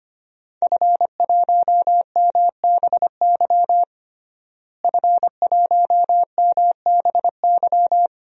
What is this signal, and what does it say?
Morse code, heard as a keyed tone, about 25 wpm, 700 Hz. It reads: F1MBY F1MBY
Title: DXCC-100 Call Signs - 5 Characters - Repeated 3x - 25wpm